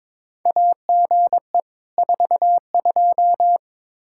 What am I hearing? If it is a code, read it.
Morse code, heard as a keyed tone, about 22 wpm, 700 Hz. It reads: AGE 42
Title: Practice QSO - Exchange 3 - More Info - 22wpm